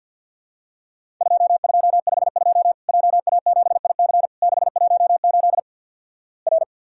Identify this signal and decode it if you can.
Morse code, heard as a keyed tone, about 50 wpm, 700 Hz. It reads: 2252 JA7IC 618 R
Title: Worked All Europe DX Contest - QTC Exchange with Repeats - 50wpm